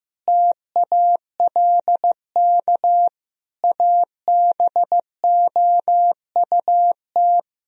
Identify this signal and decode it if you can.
Morse code, heard as a keyed tone, about 15 wpm, 700 Hz. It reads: TALK ABOUT
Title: Sentences from Top 1000 Words - Review Part 4 - 15wpm